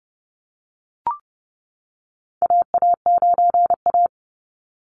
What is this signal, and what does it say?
Morse code, heard as a keyed tone, about 30 wpm, 700 Hz. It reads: AA9A
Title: Call Signs - 30wpm